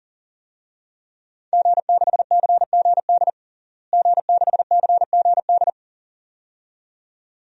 Morse code, heard as a keyed tone, about 37 wpm, 700 Hz. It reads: G6CGD G6CGD
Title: DXCC-100 Call Signs - 5 Characters - Repeated 2x - 40wpm